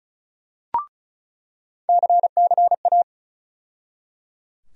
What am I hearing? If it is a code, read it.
Morse code, heard as a keyed tone, about 34 wpm, 700 Hz. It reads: CCA
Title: USA Call Sign Postfixes - 3 Letters 35wpm